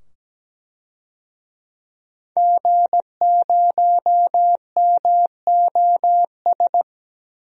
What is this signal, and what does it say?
Morse code, heard as a keyed tone, about 17 wpm, 700 Hz. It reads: G0MOS
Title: DXCC-100 Call Signs - 5 Characters - 17wpm